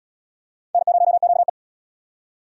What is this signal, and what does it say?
Morse code, heard as a keyed tone, about 75 wpm, 700 Hz. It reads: N0QE